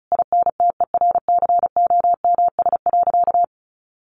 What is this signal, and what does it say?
Morse code, heard as a keyed tone, about 34 wpm, 700 Hz. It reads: INTERCOMS.
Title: A Sound Of Thunder 35wpm